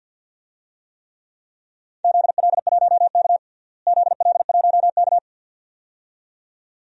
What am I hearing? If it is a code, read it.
Morse code, heard as a keyed tone, about 51 wpm, 700 Hz. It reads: ZL1K ZL1K
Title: DXCC-100 Call Signs - 4 Characters - Repeated 2x - 50wpm